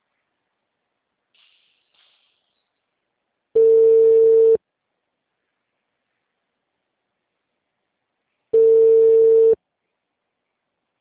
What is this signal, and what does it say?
Ringback tone (China, 450 Hz), with speech.